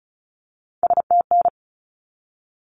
Morse code, heard as a keyed tone, about 34 wpm, 700 Hz. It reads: STN
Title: Abbreviations 35wpm